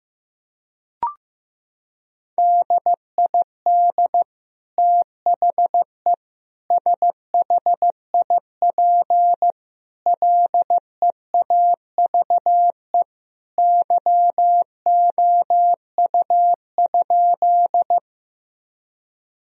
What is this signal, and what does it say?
Morse code, heard as a keyed tone, about 15 wpm, 700 Hz. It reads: DID THE SHIP LEAVE YOU?
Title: Sentences from Top 1000 Words - Review Part 2 - 15wpm